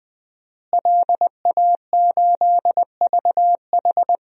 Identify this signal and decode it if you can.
Morse code, heard as a keyed tone, about 21 wpm, 700 Hz. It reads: LA8VH
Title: DXCC-100 Call Signs - 5 Characters - Repeated 2x - 20wpm